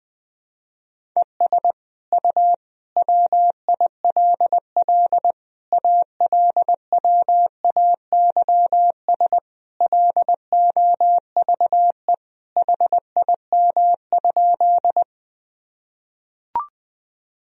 Morse code, heard as a keyed tone, about 21 wpm, 700 Hz. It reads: ES U WILL ALWAYS LOVE HIM?
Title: Sentences from Top 400 Words and Abbreviations 20wpm